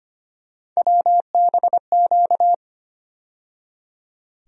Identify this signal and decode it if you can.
Morse code, heard as a keyed tone, about 25 wpm, 700 Hz. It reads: WBQ
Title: USA Call Sign Postfixes - 3 Letters 25wpm